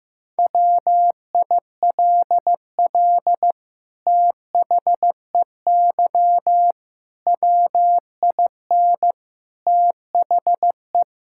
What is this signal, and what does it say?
Morse code, heard as a keyed tone, about 15 wpm, 700 Hz. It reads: WILL THEY WIN THE
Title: Sentences from Top 1600 Words 15wpm